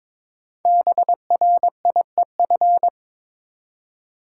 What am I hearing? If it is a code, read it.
Morse code, heard as a keyed tone, about 22 wpm, 700 Hz. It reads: BRIEF
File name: 5 Letter Words - First 3 Letters Given - 22wpm